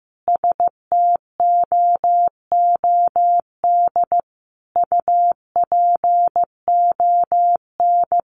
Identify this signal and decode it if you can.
Morse code, heard as a keyed tone, about 15 wpm, 700 Hz. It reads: STOOD UPON
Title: Sentences from Top 1100 Words 15wpm